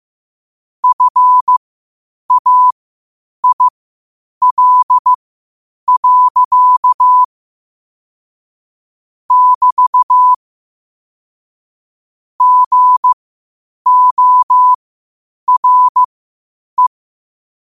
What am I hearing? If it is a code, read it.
Morse code, heard as a keyed tone, about 15 wpm, spaced as for 8 wpm, 1000 Hz. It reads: FAIL. <BT> GORE
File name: qotdfiles / 8wpm